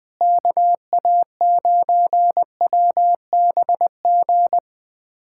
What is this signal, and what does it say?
Morse code, heard as a keyed tone, about 20 wpm, 700 Hz. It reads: KA9WBG
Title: USA Callsign DE Callsign - 6 Characters - 20wpm